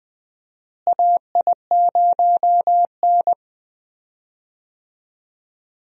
Morse code, heard as a keyed tone, about 20 wpm, 700 Hz. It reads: AI0N
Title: USA Call Signs - 4 Characters - Repeated 2x - 20wpm